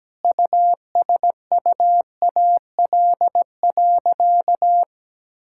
Morse code, heard as a keyed tone, about 17 wpm, 700 Hz. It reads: USUAL.
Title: Sentences from Top 1000 Words - Review Part 6 - 17wpm